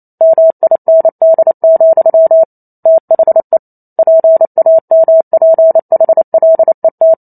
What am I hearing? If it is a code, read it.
Morse code, heard as a keyed tone, about 28 wpm, 650 Hz. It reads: MIND, THE PAMPHLET